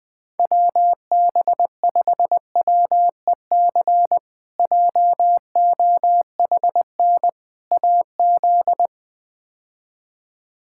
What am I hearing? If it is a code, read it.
Morse code, heard as a keyed tone, about 20 wpm, 700 Hz. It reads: WB5WEC JOHN AZ
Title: K1USN SST QSO Practice - 20wpm